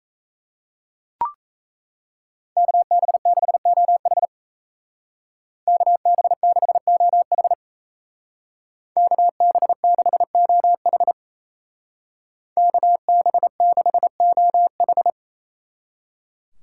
Morse code, heard as a keyed tone, about 37 wpm, 700 Hz. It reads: KB6OH KB6OH KB6OH KB6OH